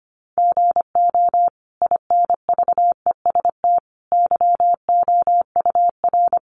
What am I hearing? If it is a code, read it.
Morse code, heard as a keyed tone, about 24 wpm, 700 Hz. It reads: GO INVEST YOUR
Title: Sentences from Top 1100 Words 25wpm